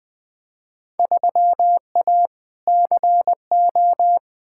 Morse code, heard as a keyed tone, about 20 wpm, 700 Hz. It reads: 3A CO
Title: Field Day Exchanges Encore - 20wpm